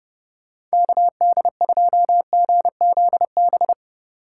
Morse code, heard as a keyed tone, about 30 wpm, 700 Hz. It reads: KD2GZB
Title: USA Call Signs - 6 Characters - 8x Character Spacing - 30wpm